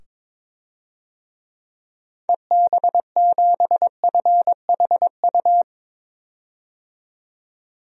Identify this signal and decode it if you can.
Morse code, heard as a keyed tone, about 22 wpm, 700 Hz. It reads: EB7FHU
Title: DXCC-100 Call Signs - 6 Characters - 22wpm